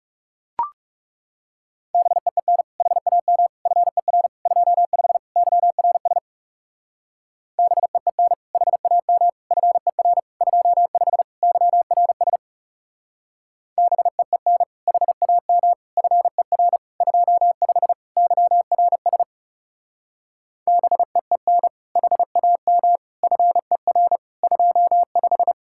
Morse code, heard as a keyed tone, about 39 wpm, 700 Hz. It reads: BEEN HAM FER 25 YRS BEEN HAM FER 25 YRS BEEN HAM FER 25 YRS BEEN HAM FER 25